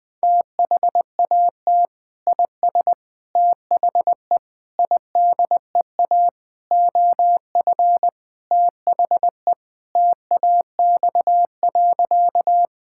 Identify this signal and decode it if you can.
Morse code, heard as a keyed tone, about 20 wpm, 700 Hz. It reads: THAT IS THE IDEA OF THE TAX.
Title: Sentences from Top 1000 Words - Review Part 5 - 20wpm